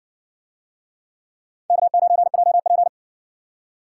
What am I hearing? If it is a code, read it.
Morse code, heard as a keyed tone, about 60 wpm, 700 Hz. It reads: K9JP